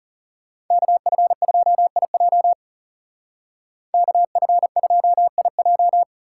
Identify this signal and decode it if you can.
Morse code, heard as a keyed tone, about 36 wpm, 700 Hz. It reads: KF2IJ KF2IJ
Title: Speed Racing - USA 2x2 Call Signs 30wpm